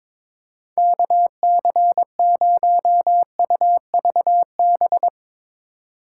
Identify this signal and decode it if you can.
Morse code, heard as a keyed tone, about 22 wpm, 700 Hz. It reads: KC0UVB
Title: Speed Racing - USA 2x3 Call Signs 22wpm